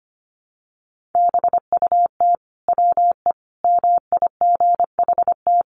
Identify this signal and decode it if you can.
Morse code, heard as a keyed tone, about 25 wpm, 700 Hz. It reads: BUT WE MIGHT
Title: Sentences from Top 1300 Words 25wpm